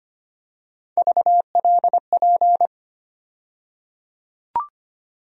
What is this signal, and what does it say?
Morse code, heard as a keyed tone, about 25 wpm, 700 Hz. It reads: VLP
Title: USA Call Sign Postfixes - 3 Letters 25wpm